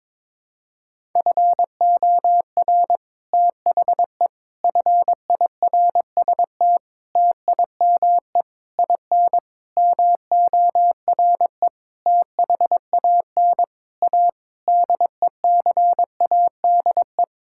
Morse code, heard as a keyed tone, about 22 wpm, 700 Hz. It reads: FOR THE FIRST TIME IN MORE THAN A DECADE
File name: Sets of 9 Words - 22wpm